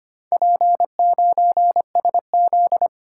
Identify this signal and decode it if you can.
Morse code, heard as a keyed tone, about 25 wpm, 700 Hz. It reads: P9SZ
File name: Four Letter-Number Combinations - 25wpm